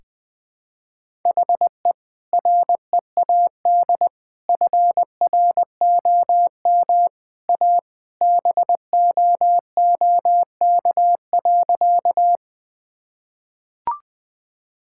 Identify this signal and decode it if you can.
Morse code, heard as a keyed tone, about 21 wpm, 700 Hz. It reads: HE READ FROM A BOOK.
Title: Sentences with 2-Word Phrases - Spoken Beginning - 20wpm